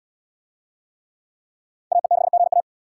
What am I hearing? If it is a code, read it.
Morse code, heard as a keyed tone, about 66 wpm, 700 Hz. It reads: AE6GA